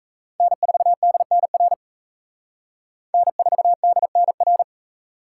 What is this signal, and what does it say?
Morse code, heard as a keyed tone, about 35 wpm, 700 Hz. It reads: N4DNR N4DNR